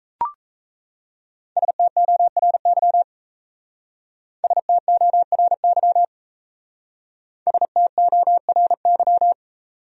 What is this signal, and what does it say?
Morse code, heard as a keyed tone, about 36 wpm, 700 Hz. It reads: STORY STORY STORY